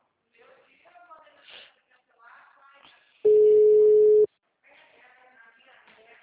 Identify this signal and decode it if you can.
Ringback tone (Europe, 425 Hz), with speech.